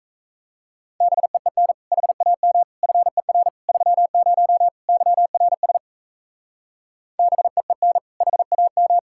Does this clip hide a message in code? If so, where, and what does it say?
Yes: Morse code from 1.0 to 9.0 s, heard as a keyed tone, about 36 wpm, 700 Hz. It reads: BEEN HAM FER 30 YRS BEEN HAM